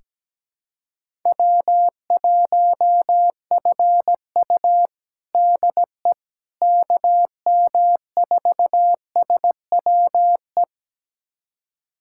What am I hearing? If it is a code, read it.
Morse code, heard as a keyed tone, about 17 wpm, 700 Hz. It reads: W1FU DE KM4SWE
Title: USA Callsign DE Callsign 17wpm